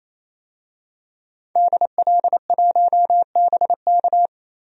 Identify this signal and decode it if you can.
Morse code, heard as a keyed tone, about 28 wpm, 700 Hz. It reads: DL1BK
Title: DXCC-100 Call Signs - 5 Characters - 28wpm